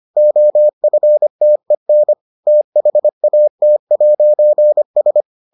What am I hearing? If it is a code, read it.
Morse code, heard as a keyed tone, about 25 wpm, 600 Hz. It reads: OFTEN THAT'S